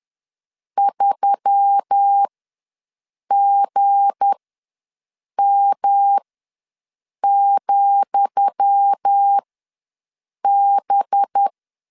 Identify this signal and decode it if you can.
Morse code, heard as a keyed tone, about 10 wpm, spaced as for 6 wpm, 800 Hz. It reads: 3GM,B
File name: LEZIONI / 10 SIGNS / 10D COMA ws 10wpm 4314